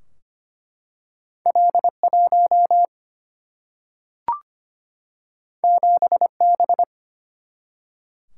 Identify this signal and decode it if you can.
Morse code, heard as a keyed tone, about 25 wpm, 700 Hz. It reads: L1 7B
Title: Single-Letter Single-Number 25wpm